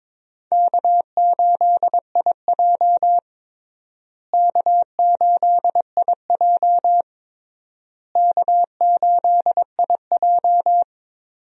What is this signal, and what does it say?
Morse code, heard as a keyed tone, about 21 wpm, 700 Hz. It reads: K8IJ K8IJ K8IJ
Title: USA Call Signs - 4 Characters - Repeated 3x - 22wpm